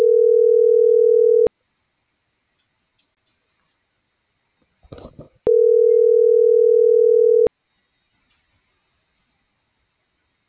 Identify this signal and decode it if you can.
Ringback tone (North America), 440 + 480 Hz.